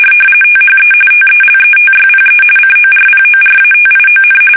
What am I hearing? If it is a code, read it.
RTTY, about 45.45 baud. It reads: THE TELETYPE CORPORATION
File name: fsk850 / teletype-corp